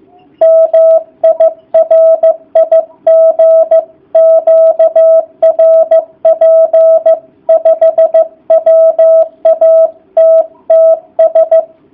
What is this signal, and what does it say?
Morse code, heard as a keyed tone, about 14 wpm, 650 Hz. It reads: MI RIG QRP 5 WATTS